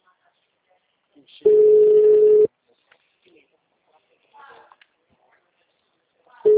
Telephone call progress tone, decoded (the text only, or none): ringback tone